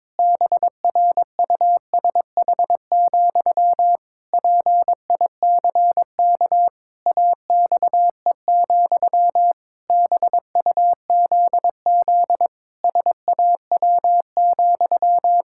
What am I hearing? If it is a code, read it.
Morse code, heard as a keyed tone, about 22 wpm, 700 Hz. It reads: BRUSH, PICK AXE, BUZZ SAW,
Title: Pangram Challenge v2 - 22wpm